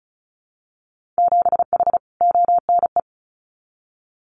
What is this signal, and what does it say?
Morse code, heard as a keyed tone, about 34 wpm, 700 Hz. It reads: 7H ONE
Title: Winter Field Day Exchanges Encore 35wpm